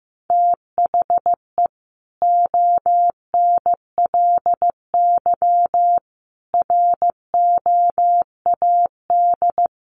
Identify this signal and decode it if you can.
Morse code, heard as a keyed tone, about 15 wpm, 700 Hz. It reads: THE ONLY ROAD
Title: Sentences from Top 1100 Words 15wpm